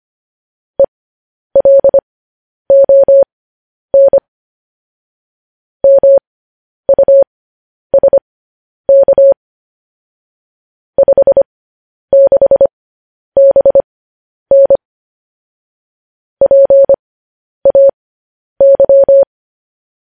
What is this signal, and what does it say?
Morse code, heard as a keyed tone, about 25 wpm, spaced as for 10 wpm, 550 Hz. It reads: ELON MUSK 56BN PAY